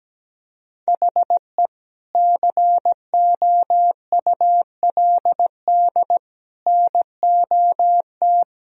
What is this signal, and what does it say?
Morse code, heard as a keyed tone, about 17 wpm, 700 Hz. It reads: HE COULD NOT
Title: Sentences from Top 1000 Words - Review Part 1 - 17wpm